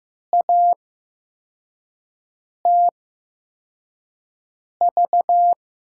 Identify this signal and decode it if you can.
Morse code, heard as a keyed tone, about 15 wpm, spaced as for 4 wpm, 700 Hz. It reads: ATV
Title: USA Call Signs - 8x Character Spacing - 15wpm